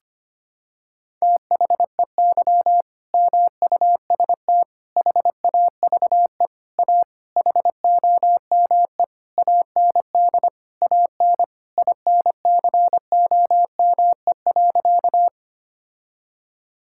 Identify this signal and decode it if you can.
Morse code, heard as a keyed tone, about 25 wpm, 700 Hz. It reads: THEY MUST HAVE A HOME AND AN INCOME.
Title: Sentences from Top 1000 Words - Review Part 5 - 25wpm